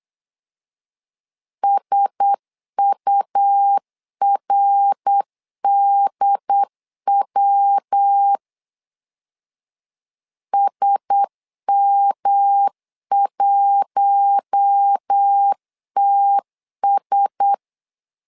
Morse code, heard as a keyed tone, about 8 wpm, 800 Hz. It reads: SURDW SM1TS
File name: LEZIONI / 4 Q1S 8wpm 2710